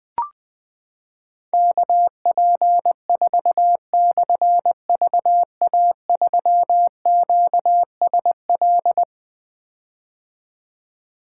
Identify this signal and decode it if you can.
Morse code, heard as a keyed tone, about 20 wpm, 700 Hz. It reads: KP4/VA3QSL
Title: Difficult Call Signs 20wpm